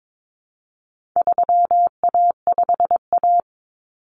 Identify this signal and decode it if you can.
Morse code, heard as a keyed tone, about 22 wpm, 700 Hz. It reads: 3A5A
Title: Common Cut Numbers - 4-Digit 22wpm